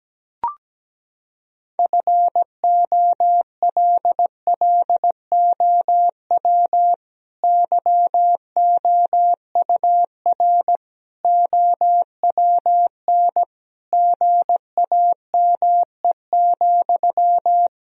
Morse code, heard as a keyed tone, about 17 wpm, 700 Hz. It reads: FOLLOW YOUR OWN GAME,
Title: Sentences from Top 1000 Words - Review Part 3 - 17wpm